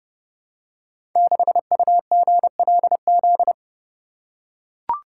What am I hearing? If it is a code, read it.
Morse code, heard as a keyed tone, about 28 wpm, 700 Hz. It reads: 6UGLZ
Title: Code Groups - 5 Letter:Numbers - 30wpm